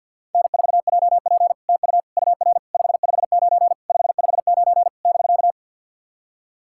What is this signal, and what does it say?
Morse code, heard as a keyed tone, about 53 wpm, 700 Hz. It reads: N4JP TU UR 559 559 <BK>
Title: SOTA QSO Practice 50wpm